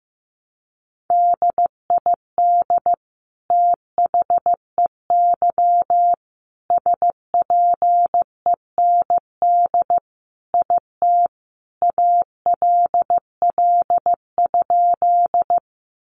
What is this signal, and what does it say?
Morse code, heard as a keyed tone, about 15 wpm, 700 Hz. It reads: DID THEY SPEND IT ALL?
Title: Sentences from Top 800 Words 15wpm